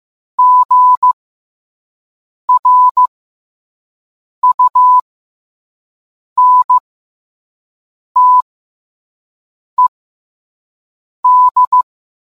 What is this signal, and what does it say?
Morse code, heard as a keyed tone, about 15 wpm, spaced as for 5 wpm, 1000 Hz. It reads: GRUNTED